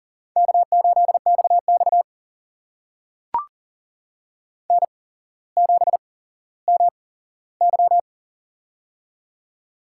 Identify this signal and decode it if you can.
Morse code, heard as a keyed tone, about 40 wpm, spaced as for 17 wpm, 700 Hz. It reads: K8XX N7MY
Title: USA Call Signs - 4 Characters - 8x Character Spacing - 40wpm